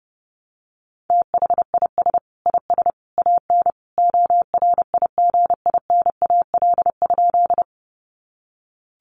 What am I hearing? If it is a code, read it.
Morse code, heard as a keyed tone, about 30 wpm, 700 Hz. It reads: THIS IS AN ORIGINAL?
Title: Sentences from Top 900 Words 30wpm